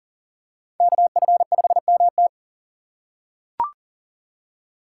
Morse code, heard as a keyed tone, about 40 wpm, 700 Hz. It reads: KF5MT
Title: USA Call Signs - 5 Characters - 2x Character Spacing - 40wpm